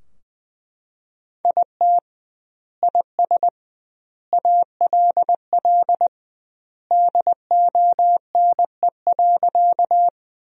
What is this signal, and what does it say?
Morse code, heard as a keyed tone, about 20 wpm, 700 Hz. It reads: IT IS ALL DONE.
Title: 2x Word-Spacing - Sentences from Top 500 Words - 20wpm